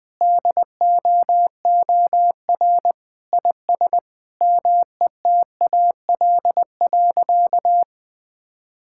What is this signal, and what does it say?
Morse code, heard as a keyed tone, about 20 wpm, 700 Hz. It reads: DOOR IS METAL.
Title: Sentences from Top 1600 Words 20wpm